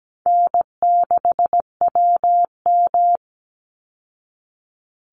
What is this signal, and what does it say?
Morse code, heard as a keyed tone, about 17 wpm, 700 Hz. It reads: N6WM
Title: Call Signs Repeated 3x - 17wpm